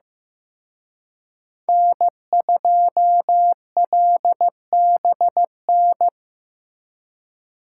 Morse code, heard as a keyed tone, about 18 wpm, 700 Hz. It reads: N2LBN
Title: Speed Racing - USA 1x3 Call Signs 15wpm